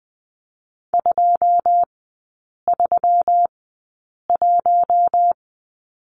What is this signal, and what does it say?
Morse code, heard as a keyed tone, about 20 wpm, spaced as for 8 wpm, 700 Hz. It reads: 231
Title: 191 Lesson 31e - Letter v - Three Characters 20wpm